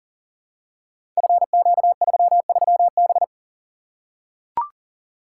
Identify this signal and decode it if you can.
Morse code, heard as a keyed tone, about 39 wpm, 700 Hz. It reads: FQ33B
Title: Code Groups - 5 Letter:Numbers - 40wpm